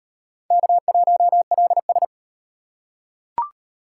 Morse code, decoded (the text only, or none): K1LS